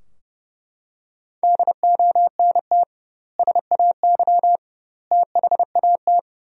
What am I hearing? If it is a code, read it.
Morse code, heard as a keyed tone, about 30 wpm, 700 Hz. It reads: DONT SAY THAT
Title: 2x Word-Spacing - Sentences from Top 300 Words - 30wpm